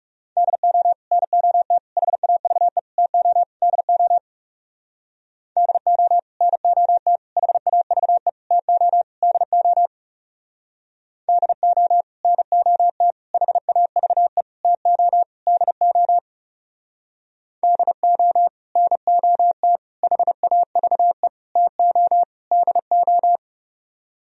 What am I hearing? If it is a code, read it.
Morse code, heard as a keyed tone, about 36 wpm, 700 Hz. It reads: DO NOT HAVE TO DO DO NOT HAVE TO DO DO NOT HAVE TO DO DO NOT HAVE TO DO
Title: Speed-Racing - Sets of 5 Words 30wpm